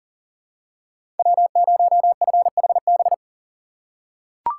Morse code, heard as a keyed tone, about 37 wpm, 700 Hz. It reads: W0FHB
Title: USA Call Signs - 5 Characters - 2x Character Spacing - 40wpm